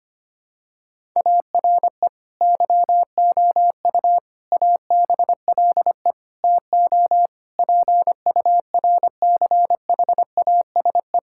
Morse code, heard as a keyed tone, about 25 wpm, 700 Hz. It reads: ARE YOU ABLE TO PURCHASE
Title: Sentences from Top 1000 Words - Review Part 5 - 25wpm